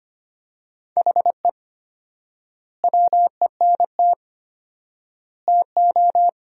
Morse code, heard as a keyed tone, about 25 wpm, 700 Hz. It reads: HE WENT TO
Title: 4x Word-Spacing - Sentences from Top 200 Words - 25wpm